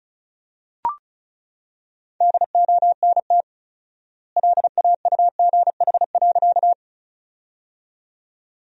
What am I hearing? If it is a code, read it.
Morse code, heard as a keyed tone, about 36 wpm, 700 Hz. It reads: DONT LAUGH.
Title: 4x Word-Spacing - Sentences from Top 500 Words - 35wpm